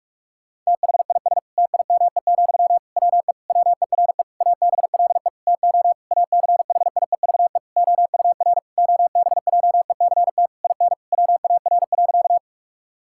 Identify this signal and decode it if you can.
Morse code, heard as a keyed tone, about 46 wpm, 700 Hz. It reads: THIS TIME, WE WERE ABLE TO ACHIEVE OUR OBJECT IN WAR.
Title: Sentences from Top 1000 Words - Review Part 6 - 45wpm